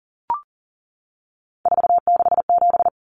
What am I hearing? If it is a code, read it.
Morse code, heard as a keyed tone, about 40 wpm, 700 Hz. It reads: 467
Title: Common Cut Numbers - 3-Digit 40wpm